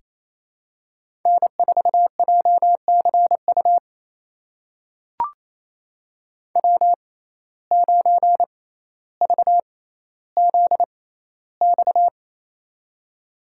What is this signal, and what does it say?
Morse code, heard as a keyed tone, about 28 wpm, spaced as for 14 wpm, 700 Hz. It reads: N4JCU W9VZX
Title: USA Call Signs - 5 Characters - 6x Character Spacing - 28wpm